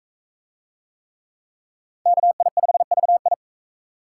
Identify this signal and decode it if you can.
Morse code, heard as a keyed tone, about 40 wpm, 700 Hz. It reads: KI5VI